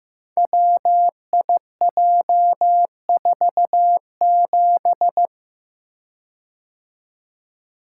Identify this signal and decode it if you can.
Morse code, heard as a keyed tone, about 15 wpm, 700 Hz. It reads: WIJ47
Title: Code Groups - 5 Letter:Numbers - 15wpm